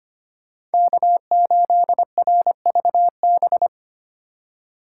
Morse code, heard as a keyed tone, about 33 wpm, 700 Hz. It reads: K8RVB